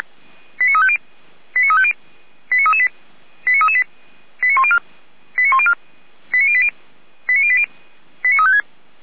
Five-tone selective call (ZVEI), heard as quick runs of five tones, tones of about 70 ms. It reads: 89350 89350 89338 89338 89104 89104 89080 89080 89356